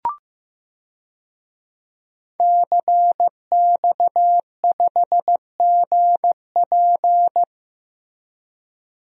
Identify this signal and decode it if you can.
Morse code, heard as a keyed tone, about 16 wpm, 700 Hz. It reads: CX5GP
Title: DXCC-100 Call Signs - 5 Characters - Repeated 2x - 15wpm